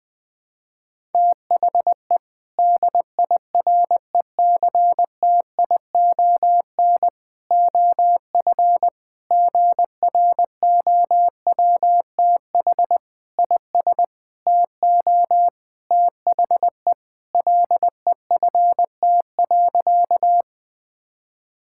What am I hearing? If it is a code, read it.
Morse code, heard as a keyed tone, about 20 wpm, 700 Hz. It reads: THE DIRECTION OF GROWTH IS TO THE LEFT.
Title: Sentences from Top 1000 Words - Review Part 5 - 20wpm